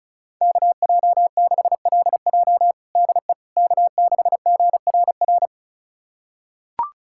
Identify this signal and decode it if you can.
Morse code, heard as a keyed tone, about 34 wpm, 700 Hz. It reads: KJ6LJ DE K6GRR
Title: USA Callsign DE Callsign - 5 Characters - 35wpm